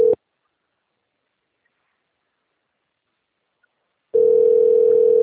Ringback tone (North America), 440 + 480 Hz.
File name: rg-1600-unknown- - - .65525